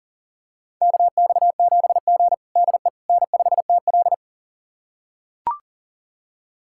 Morse code, heard as a keyed tone, about 40 wpm, 700 Hz. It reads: KX7G DE N5TL
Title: USA Callsign DE Callsign - 4 Characters - 40wpm